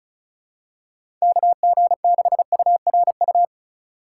Morse code, heard as a keyed tone, about 34 wpm, 700 Hz. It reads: KG6URU
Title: USA Call Signs - 4x Character Spacing - 35wpm